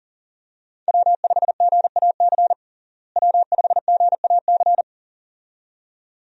Morse code, heard as a keyed tone, about 37 wpm, 700 Hz. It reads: W5GAC W5GAC
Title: USA Call Signs - 5 Characters - Repeated 2x - 40wpm